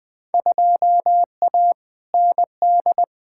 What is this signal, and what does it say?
Morse code, heard as a keyed tone, about 20 wpm, 700 Hz. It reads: 2A ND
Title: Field Day Exchanges - 20wpm